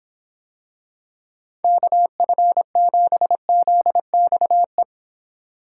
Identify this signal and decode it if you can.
Morse code, heard as a keyed tone, about 26 wpm, 700 Hz. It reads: KF7ZXE